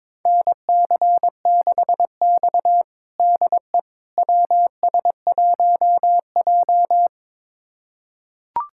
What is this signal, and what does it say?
Morse code, heard as a keyed tone, about 22 wpm, 700 Hz. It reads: NC6X DE WS1J
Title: USA Callsign DE Callsign - 4 Characters - 22wpm